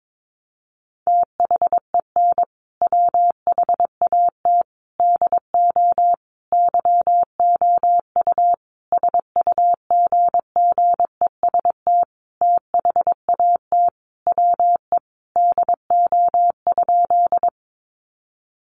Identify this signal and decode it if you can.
Morse code, heard as a keyed tone, about 22 wpm, 700 Hz. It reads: THEN WHAT DO YOU SUGGEST THAT WE DO?
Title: Sentences from Top 1300 Words 22wpm